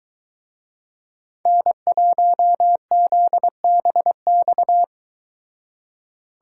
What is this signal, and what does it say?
Morse code, heard as a keyed tone, about 23 wpm, 700 Hz. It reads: N1ZBX